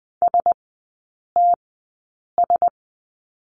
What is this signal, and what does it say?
Morse code, heard as a keyed tone, about 20 wpm, spaced as for 8 wpm, 700 Hz. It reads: STS
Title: 035 Lesson 7e - Letter s - Three Characters 20wpm